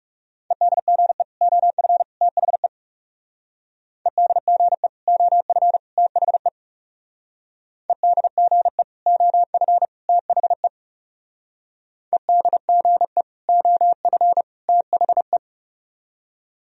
Morse code, heard as a keyed tone, about 37 wpm, 700 Hz. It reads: EDGE OF THE EDGE OF THE EDGE OF THE EDGE OF THE
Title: Speed-Racing - Sets of 3 Words 30wpm